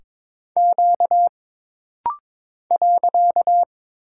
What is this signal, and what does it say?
Morse code, heard as a keyed tone, about 22 wpm, 700 Hz. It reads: Q .
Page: This rapid-fire timing is geared to push your ICR (Instant Character Recognition) skills.